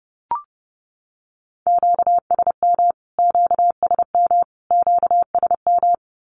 Morse code, heard as a keyed tone, about 29 wpm, 700 Hz. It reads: QSM QSM QSM
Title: Q-Codes Repeated 3x 30wpm